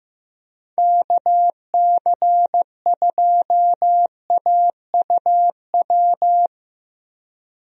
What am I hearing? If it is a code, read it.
Morse code, heard as a keyed tone, about 15 wpm, 700 Hz. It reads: KC2AUW
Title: USA Call Signs - 6 Characters - Repeated 2x - 15wpm